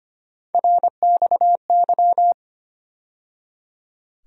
Morse code, heard as a keyed tone, about 25 wpm, 700 Hz. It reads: RXY
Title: USA Call Sign Postfixes - 3 Letters 25wpm